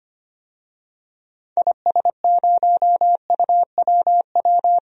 Morse code, heard as a keyed tone, about 25 wpm, 700 Hz. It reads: IS0UWW
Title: DXCC-100 Call Signs - 6 Characters - 25wpm